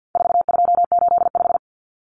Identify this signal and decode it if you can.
Morse code, heard as a keyed tone, about 50 wpm, 700 Hz. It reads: 4285
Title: Common Cut Numbers - 4-Digit 50wpm